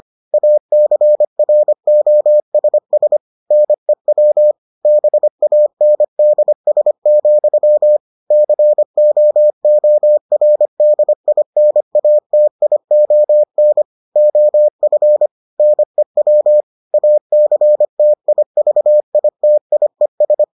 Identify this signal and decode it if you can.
Morse code, heard as a keyed tone, about 25 wpm, 600 Hz. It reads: ACROSS NEW BANDS, COORDINATION OF NEW ACTIVITIES